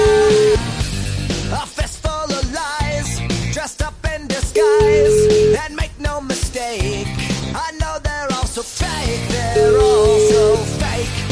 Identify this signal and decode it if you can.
Ringback tone (Europe, 425 Hz), with noise.